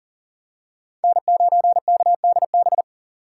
Morse code, heard as a keyed tone, about 40 wpm, 700 Hz. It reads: N9KDB
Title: K1USN SST QSO Practice - 40wpm